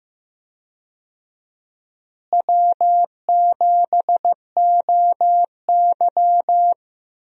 Morse code, heard as a keyed tone, about 15 wpm, 700 Hz. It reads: W7OY
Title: USA Call Signs - 4 Characters - Repeated 3x - 15wpm